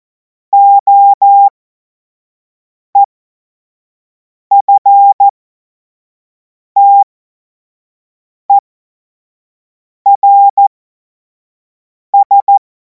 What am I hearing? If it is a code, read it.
Morse code, heard as a keyed tone, about 14 wpm, spaced as for 5 wpm, 800 Hz. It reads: OEFTERS
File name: Morsekurs / Witze / Witz 17. März 2026 25 F70